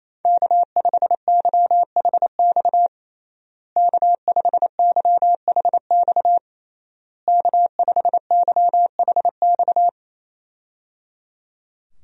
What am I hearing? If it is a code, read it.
Morse code, heard as a keyed tone, about 28 wpm, 700 Hz. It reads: K5YHX K5YHX K5YHX
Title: USA Call Signs - 5 Characters - Repeated 3x - 28wpm